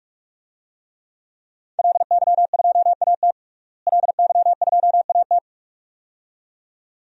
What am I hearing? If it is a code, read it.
Morse code, heard as a keyed tone, about 45 wpm, 700 Hz. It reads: LY2AT LY2AT
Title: DXCC-100 Call Signs - 5 Characters - Repeated 2x - 45wpm